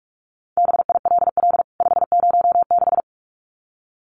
Morse code, heard as a keyed tone, about 44 wpm, 700 Hz. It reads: BILL 506
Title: CWT Exchange 45wpm